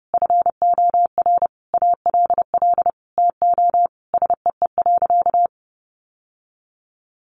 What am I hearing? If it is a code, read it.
Morse code, heard as a keyed tone, about 30 wpm, 700 Hz. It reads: FOR ALL TO SEE.
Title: Sentences from Top 1200 Words 30wpm